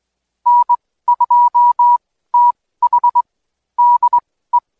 Morse code, heard as a keyed tone, about 21 wpm, spaced as for 15 wpm, 970 Hz. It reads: N2TH DE
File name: Code Quick / AudioDownld2 / Seg125-DayDrill30-16WPM